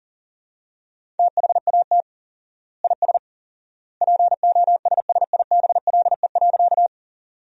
Morse code, heard as a keyed tone, about 40 wpm, 700 Hz. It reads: THAT IS POSSIBLE.
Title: 4x Word-Spacing - Sentences from Top 500 Words - 40wpm